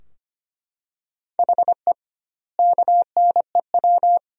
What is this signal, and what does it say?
Morse code, heard as a keyed tone, about 25 wpm, 700 Hz. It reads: HE KNEW
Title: 2x Word-Spacing - Sentences from Top 500 Words - 25wpm